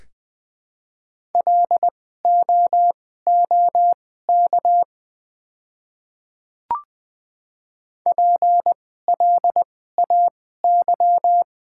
Morse code, heard as a keyed tone, about 20 wpm, spaced as for 14 wpm, 700 Hz. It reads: LOOK PLAY
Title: 2x Character Spacing - Top 300 Words - 20wpm